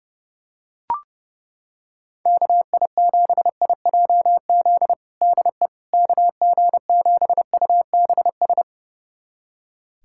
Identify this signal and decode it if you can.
Morse code, heard as a keyed tone, about 30 wpm, 700 Hz. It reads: KI7IJZ DE KG7UBS
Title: USA Callsign DE Callsign 30wpm